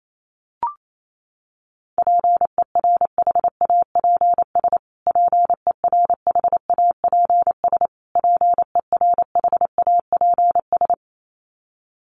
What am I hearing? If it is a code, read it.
Morse code, heard as a keyed tone, about 28 wpm, 700 Hz. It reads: PERHAPS PERHAPS PERHAPS
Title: Top 500 Words Repeated 28wpm